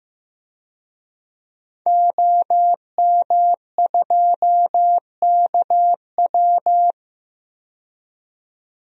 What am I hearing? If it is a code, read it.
Morse code, heard as a keyed tone, about 15 wpm, 700 Hz. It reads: OM2KW
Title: DXCC-100 Call Signs - 5 Characters - 15wpm